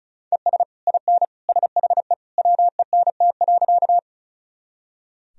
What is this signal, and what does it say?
Morse code, heard as a keyed tone, about 36 wpm, 690 Hz. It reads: ES IN SHE WENT.
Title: Sentences from Top 200 Words and Abbreviations 35wpm